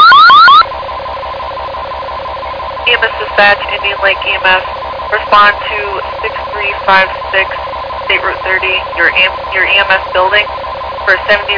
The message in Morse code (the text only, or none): none